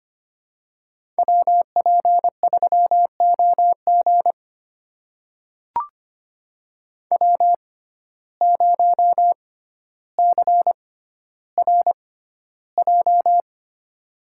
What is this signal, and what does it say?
Morse code, heard as a keyed tone, about 25 wpm, spaced as for 13 wpm, 700 Hz. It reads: WP3OG W0CRJ
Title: USA Call Signs - 5 Characters - 6x Character Spacing - 25wpm